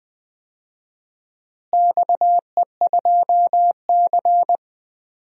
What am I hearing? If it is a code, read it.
Morse code, heard as a keyed tone, about 20 wpm, 700 Hz. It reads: XE2C
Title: DXCC-100 Call Signs - 4 Characters - Repeated 3x - 20wpm